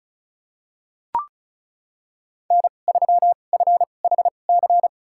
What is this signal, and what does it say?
Morse code, heard as a keyed tone, about 36 wpm, spaced as for 25 wpm, 700 Hz. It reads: N3FHC
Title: USA Call Signs - 5 Characters - 2x Character Spacing - 35wpm